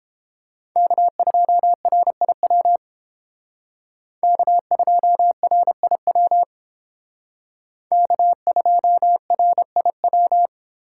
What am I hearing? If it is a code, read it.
Morse code, heard as a keyed tone, about 28 wpm, 700 Hz. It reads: K2RIW K2RIW K2RIW